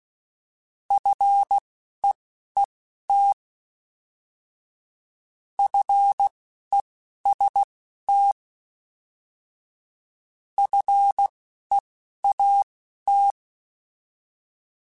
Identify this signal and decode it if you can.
Morse code, heard as a keyed tone, about 16 wpm, spaced as for 11 wpm, 800 Hz. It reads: FEET FEST FEAT
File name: Alphabetical / Alphabetical mp3s / Fs @10